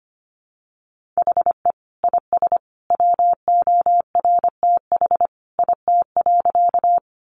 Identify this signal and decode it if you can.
Morse code, heard as a keyed tone, about 25 wpm, 700 Hz. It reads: HE IS WORTH IT.
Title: Sentences from Top 800 Words 25wpm